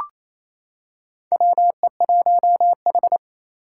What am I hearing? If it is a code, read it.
Morse code, heard as a keyed tone, about 28 wpm, 700 Hz. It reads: WE1H
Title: Four Challenging Letter-Number Combinations 28wpm